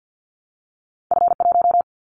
Morse code, heard as a keyed tone, about 50 wpm, 700 Hz. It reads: F1
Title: Callsign Prefixes for Top 50 DXCC - 50wpm